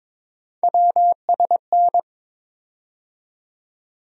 Morse code, heard as a keyed tone, about 22 wpm, 700 Hz. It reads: WSN
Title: USA Call Sign Postfixes - 3 Letters 22wpm